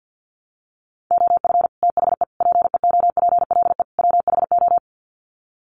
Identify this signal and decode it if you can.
Morse code, heard as a keyed tone, about 50 wpm, 700 Hz. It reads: OF THE PEOPLE WHO
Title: Sets of 4 Words 50wpm